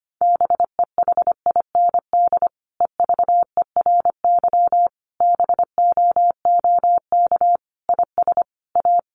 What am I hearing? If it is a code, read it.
Morse code, heard as a keyed tone, about 25 wpm, 700 Hz. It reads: BEHIND EVERY BOOK IS A
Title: Sentences from Top 500 Words 25wpm